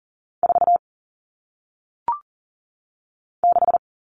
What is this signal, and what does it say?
Morse code, heard as a keyed tone, about 40 wpm, 700 Hz. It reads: 4 6
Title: All Cut Numbers - 1-Digit 40wpm